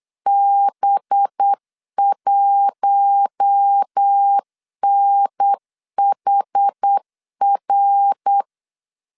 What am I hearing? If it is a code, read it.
Morse code, heard as a keyed tone, about 8 wpm, 800 Hz. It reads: B1NHR
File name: LEZIONI / 6 PKX0 / 6 P 8wpm 3025